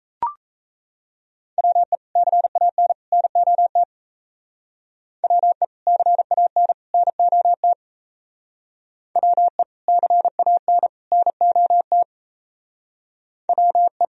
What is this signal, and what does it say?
Morse code, heard as a keyed tone, about 37 wpm, 700 Hz. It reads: WE CAN NOT WE CAN NOT WE CAN NOT WE